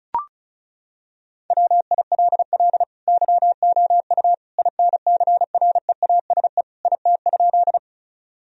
Morse code, heard as a keyed tone, about 36 wpm, 700 Hz. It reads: WILL YOU INCREASE IT?
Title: Sentences from Top 1000 Words - Review Part 4 - 35wpm